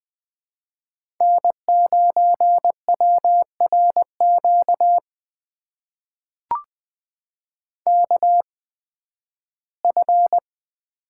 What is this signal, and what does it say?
Morse code, heard as a keyed tone, about 20 wpm, spaced as for 13 wpm, 700 Hz. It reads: N9WRQ KF
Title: USA Call Signs - 8x Character Spacing - 20wpm